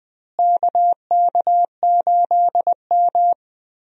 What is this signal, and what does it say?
Morse code, heard as a keyed tone, about 20 wpm, 700 Hz. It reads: KK8M
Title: USA Call Signs - 4 Characters - Repeated 3x - 20wpm